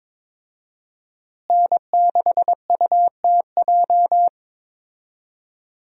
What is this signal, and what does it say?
Morse code, heard as a keyed tone, about 22 wpm, 700 Hz. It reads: N6UTJ
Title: USA Call Signs - 5 Characters - Repeated 2x - 22wpm